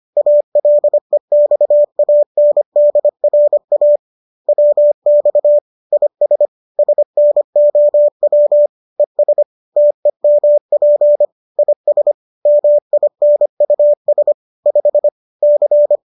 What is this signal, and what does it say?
Morse code, heard as a keyed tone, about 25 wpm, 590 Hz. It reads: ALEXANDRA WX IS SNOW ES TEMP IS MINUS 5 C
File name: qso219 25